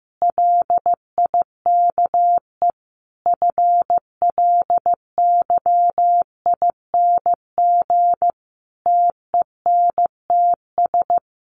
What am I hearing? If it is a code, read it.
Morse code, heard as a keyed tone, about 15 wpm, 700 Hz. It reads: LIKE FLYING TENTS
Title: A Sound Of Thunder 15wpm